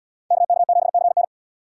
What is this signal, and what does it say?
Morse code, heard as a keyed tone, about 72 wpm, 700 Hz. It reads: KK7CN